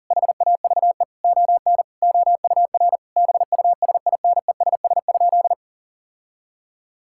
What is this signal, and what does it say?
Morse code, heard as a keyed tone, about 37 wpm, 700 Hz. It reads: HAVE ON OUR BUSINESS?
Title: Sentences from Top 1000 Words - Review Part 6 - 40wpm